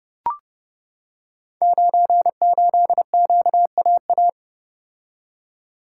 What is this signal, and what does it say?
Morse code, heard as a keyed tone, about 30 wpm, 700 Hz. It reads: 98QAA
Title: Code Groups - 5 Letter:Numbers - 30wpm